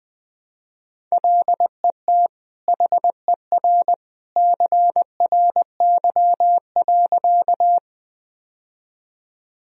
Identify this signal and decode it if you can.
Morse code, heard as a keyed tone, about 21 wpm, 700 Hz. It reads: LET HER CRY.
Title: Sentences from Top 1000 Words - Review Part 2 - 20wpm